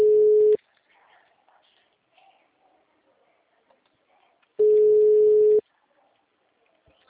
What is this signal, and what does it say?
Ringback tone (Europe, 425 Hz).
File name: out- -2018- - - .1341